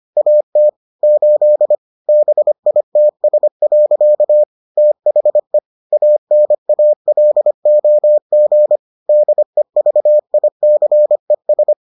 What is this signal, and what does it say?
Morse code, heard as a keyed tone, about 25 wpm, 600 Hz. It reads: AT 8 BITS. THE ANALOG DEVICES